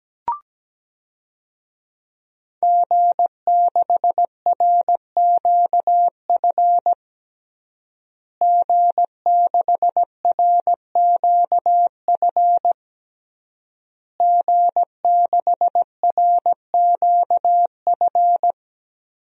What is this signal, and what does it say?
Morse code, heard as a keyed tone, about 17 wpm, 700 Hz. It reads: G6RQF G6RQF G6RQF
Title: DXCC-100 Call Signs - 5 Characters - Repeated 3x - 17wpm